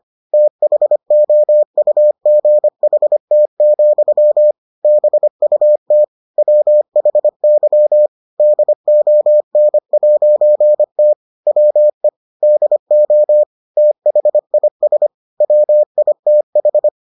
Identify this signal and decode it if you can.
Morse code, heard as a keyed tone, about 25 wpm, 600 Hz. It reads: THOUGHT, BUT WHY DON'T WE DO THIS WITH